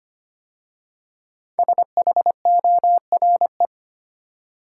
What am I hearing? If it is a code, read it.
Morse code, heard as a keyed tone, about 25 wpm, 700 Hz. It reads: SHORE
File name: 5 Letter Words - First 3 Letters Given - 25wpm